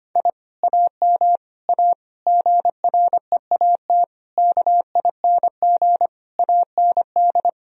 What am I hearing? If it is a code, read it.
Morse code, heard as a keyed tone, about 25 wpm, 700 Hz. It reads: I AM A GREAT KING AND
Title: Sentences from Top 1000 Words - Review Part 4 - 25wpm